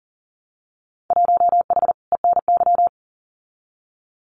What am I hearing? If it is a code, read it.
Morse code, heard as a keyed tone, about 38 wpm, 700 Hz. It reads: 1H ENY
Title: Winter Field Day Exchanges 40wpm